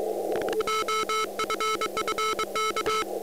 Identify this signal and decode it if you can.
Morse code, heard as a keyed tone, about 23 wpm, 1310 Hz, with noise. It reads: OFFK
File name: iw0ffk / 2013 11 02 OM6A IW0FFK OM side